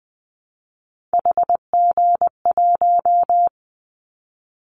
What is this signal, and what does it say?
Morse code, heard as a keyed tone, about 20 wpm, 700 Hz. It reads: HG1
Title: Callsign Prefixes for Top 25 DXCC - Repeated 3x - 20wpm